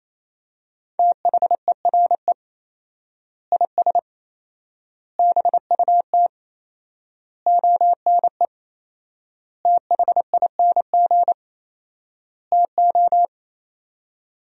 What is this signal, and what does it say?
Morse code, heard as a keyed tone, about 28 wpm, 700 Hz. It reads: THERE IS BUT ONE THING TO
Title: 4x Word-Spacing - Sentences from Top 100 Words - 28wpm